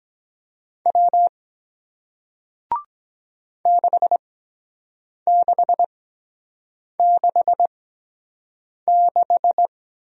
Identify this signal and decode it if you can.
Morse code, heard as a keyed tone, about 23 wpm, spaced as for 7 wpm, 700 Hz. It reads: W 6666